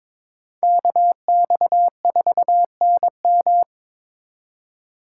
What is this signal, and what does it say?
Morse code, heard as a keyed tone, about 22 wpm, 700 Hz. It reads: KX4NM
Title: K1USN SST QSO Practice - 22wpm